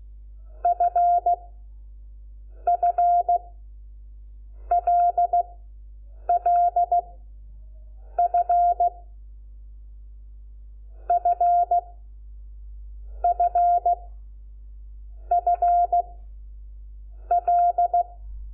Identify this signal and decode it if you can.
Morse code, heard as a keyed tone, about 16 wpm, 700 Hz. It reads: F F L L F F F F L
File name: Lecciones / CLASE Nº5